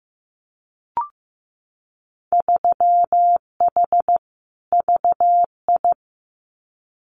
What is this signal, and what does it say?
Morse code, heard as a keyed tone, about 15 wpm, 700 Hz. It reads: 3H VI
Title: Winter Field Day Exchanges Encore 15wpm